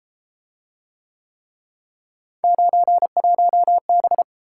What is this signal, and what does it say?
Morse code, heard as a keyed tone, about 30 wpm, 700 Hz. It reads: 91B